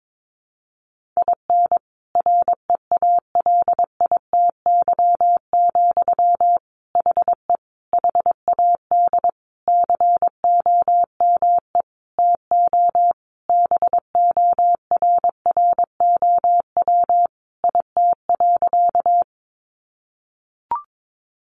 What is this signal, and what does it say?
Morse code, heard as a keyed tone, about 22 wpm, 700 Hz. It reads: IN REALITY, HE HAD COME TO BORROW IT.
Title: Sentences from Top 1500 Words 22wpm